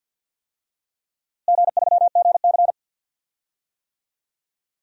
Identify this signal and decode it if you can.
Morse code, heard as a keyed tone, about 52 wpm, 700 Hz. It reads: G3GC
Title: DXCC-100 Call Signs - 4 Characters - Repeated 3x - 50wpm